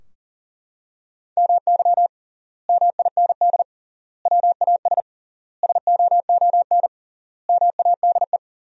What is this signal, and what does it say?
Morse code, heard as a keyed tone, about 40 wpm, 700 Hz. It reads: MY MIND WAS SOON MADE
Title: 3x Word-Spacing - Sentences from Top 500 Words - 40wpm